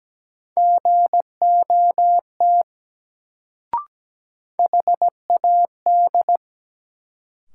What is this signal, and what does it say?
Morse code, heard as a keyed tone, about 17 wpm, 700 Hz. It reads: GOT HAD
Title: Sets of 2 Words - New Word Drill - 17wpm